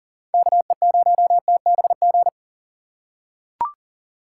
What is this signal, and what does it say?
Morse code, heard as a keyed tone, about 40 wpm, 700 Hz. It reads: KE0TBG
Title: USA Call Signs - 6 Characters - 4x Character Spacing - 40wpm